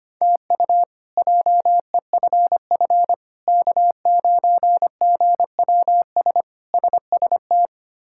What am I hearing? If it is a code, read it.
Morse code, heard as a keyed tone, about 25 wpm, 700 Hz. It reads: TU JEFF K9GWS SST
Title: K1USN SST QSO Practice - 25wpm